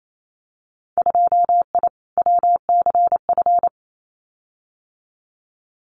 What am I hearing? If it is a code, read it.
Morse code, heard as a keyed tone, about 28 wpm, 700 Hz. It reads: 2I WCF
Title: Winter Field Day Exchanges 28wpm